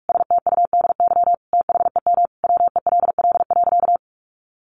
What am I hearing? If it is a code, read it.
Morse code, heard as a keyed tone, about 44 wpm, 700 Hz. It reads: STUDY THEM WELL.
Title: Sentences from Top 300 Words 45wpm